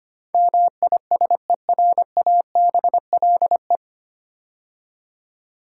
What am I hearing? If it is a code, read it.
Morse code, heard as a keyed tone, about 25 wpm, 690 Hz. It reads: MISERABLE
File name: practice / 25wpm / 9 Letter Words - First 3 Letters Given - 25wpm